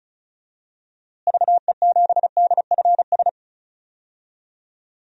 Morse code, heard as a keyed tone, about 36 wpm, 700 Hz. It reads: VE7DFS
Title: DXCC-100 Call Signs - 6 Characters - Repeated 3x - 35wpm